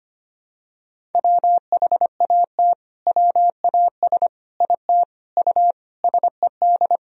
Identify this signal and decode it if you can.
Morse code, heard as a keyed tone, about 25 wpm, 700 Hz. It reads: WHAT WAS IT U SED
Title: Sentences from Top 200 Words and Abbreviations 25wpm